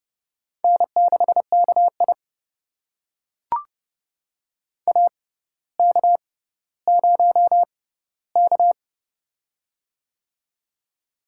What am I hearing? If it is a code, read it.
Morse code, heard as a keyed tone, about 30 wpm, spaced as for 15 wpm, 700 Hz. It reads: N6KI AK0K
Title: USA Call Signs - 4 Characters - 6x Character Spacing - 30wpm